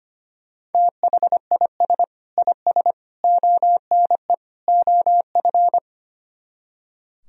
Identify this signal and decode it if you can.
Morse code, heard as a keyed tone, about 32 wpm, 700 Hz. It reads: THIS IS ONE OF